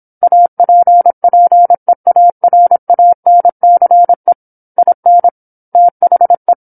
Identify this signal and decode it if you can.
Morse code, heard as a keyed tone, about 27 wpm, 700 Hz. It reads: APPEARANCE IN THE
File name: 08-Chapter-08 26 WPM